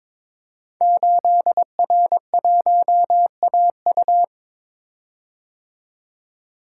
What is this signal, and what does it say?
Morse code, heard as a keyed tone, about 22 wpm, 700 Hz. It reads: 8R1AU
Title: Code Groups - 5 Letter:Numbers - 22wpm